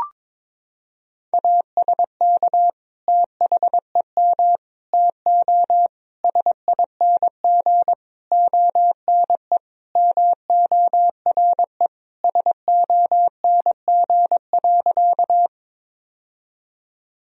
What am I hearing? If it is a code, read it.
Morse code, heard as a keyed tone, about 22 wpm, 700 Hz. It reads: ASK THEM TO SING ONE MORE SONG.
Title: Sentences from Top 1000 Words - Review Part 2 - 22wpm